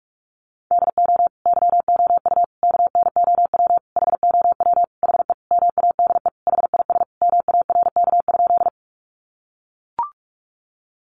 Morse code, heard as a keyed tone, about 44 wpm, 700 Hz. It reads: DO YOU KNOW HOW HE MADE HIS MARK?
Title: Sentences from Top 300 Words 45wpm